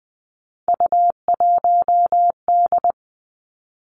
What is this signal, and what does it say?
Morse code, heard as a keyed tone, about 20 wpm, 700 Hz. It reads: U1D
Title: Three Challenging Letter-Number Combinations 20wpm